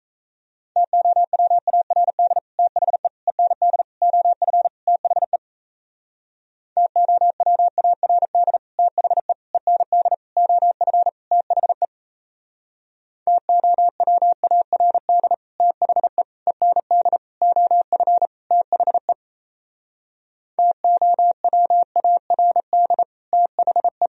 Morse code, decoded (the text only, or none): TOWARD THE END OF THE TOWARD THE END OF THE TOWARD THE END OF THE TOWARD THE